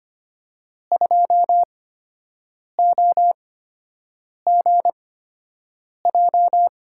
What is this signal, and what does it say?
Morse code, heard as a keyed tone, about 25 wpm, spaced as for 7 wpm, 700 Hz. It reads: 2OGJ
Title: USA Call Signs - 5 Characters - 8x Character Spacing - 25wpm